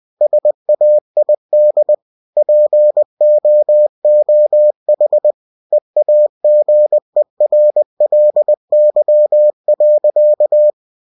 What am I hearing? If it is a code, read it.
Morse code, heard as a keyed tone, about 20 wpm, 600 Hz. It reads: SAID POOH EAGERLY.